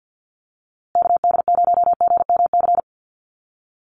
Morse code, heard as a keyed tone, about 50 wpm, 700 Hz. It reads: KD0GMC
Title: USA Call Signs Repeated 3x 50wpm